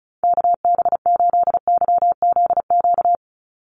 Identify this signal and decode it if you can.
Morse code, heard as a keyed tone, about 34 wpm, 700 Hz. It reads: KB8YZQ
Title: USA Call Signs 35wpm